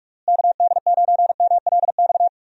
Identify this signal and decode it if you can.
Morse code, heard as a keyed tone, about 43 wpm, 700 Hz. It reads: KD9MLX
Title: USA Call Signs - 6 Characters - Repeated 3x - 45wpm